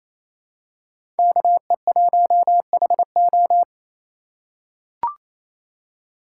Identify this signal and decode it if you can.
Morse code, heard as a keyed tone, about 28 wpm, 700 Hz. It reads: KE1HO
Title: USA Call Signs - 5 Characters - 4x Character Spacing - 28wpm